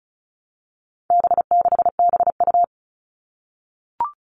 Morse code, heard as a keyed tone, about 34 wpm, 700 Hz. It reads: B6BU
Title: All Cut Numbers - 4-Digit 35wpm